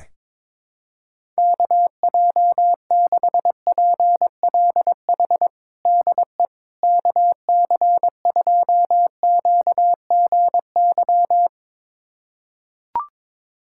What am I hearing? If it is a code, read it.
Morse code, heard as a keyed tone, about 22 wpm, 700 Hz. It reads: KJ6PLH DE KC2QGY
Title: USA Callsign DE Callsign - 6 Characters - 22wpm